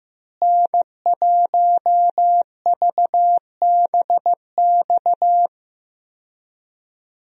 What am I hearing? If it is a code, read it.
Morse code, heard as a keyed tone, about 15 wpm, 700 Hz. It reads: N1VBX
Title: USA Call Signs - 5 Characters - Repeated 2x - 15wpm